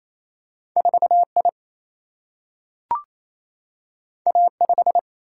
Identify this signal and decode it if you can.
Morse code, heard as a keyed tone, about 28 wpm, 700 Hz. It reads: 4 I A 5
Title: Single-Letter Single-Number 28wpm